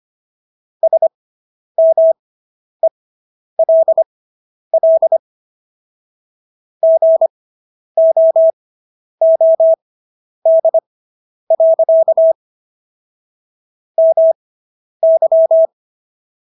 Morse code, heard as a keyed tone, about 25 wpm, spaced as for 10 wpm, 650 Hz. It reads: SMELL GOOD. MY